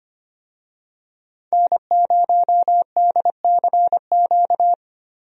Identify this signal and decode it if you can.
Morse code, heard as a keyed tone, about 25 wpm, 700 Hz. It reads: N0DCQ
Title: USA Call Signs - 5 Characters - 25wpm